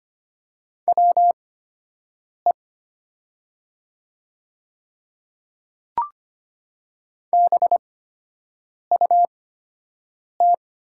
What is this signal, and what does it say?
Morse code, heard as a keyed tone, about 25 wpm, spaced as for 7 wpm, 700 Hz. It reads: WE BUT
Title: 8x Character Spacing - Top 500 Words - 25wpm